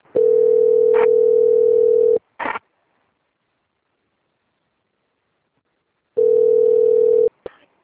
Ringback tone (North America), 440 + 480 Hz.
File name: out- - - - - .20229